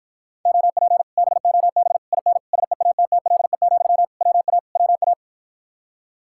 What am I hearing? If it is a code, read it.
Morse code, heard as a keyed tone, about 53 wpm, 700 Hz. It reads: OP BOB IN SEATTLE, WA WA